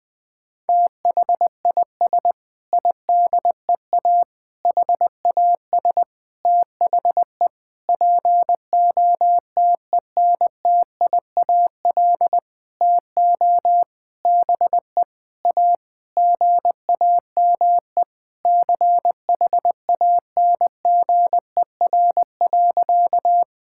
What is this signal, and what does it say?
Morse code, heard as a keyed tone, about 20 wpm, 700 Hz. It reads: THIS IDEA HAS THE POTENTIAL TO BE A GAME CHANGER.
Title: Sentences with 6-Word Phrases - Spoken Beginning - 20wpm